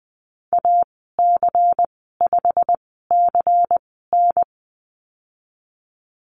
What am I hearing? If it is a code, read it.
Morse code, heard as a keyed tone, about 20 wpm, spaced as for 14 wpm, 700 Hz. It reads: AC5CN
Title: 154 Lesson 25g - Letter p - Call Signs 20wpm